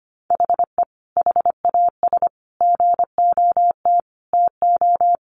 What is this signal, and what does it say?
Morse code, heard as a keyed tone, about 25 wpm, 700 Hz. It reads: HE HAS GOT TO
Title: Sentences from Top 800 Words 25wpm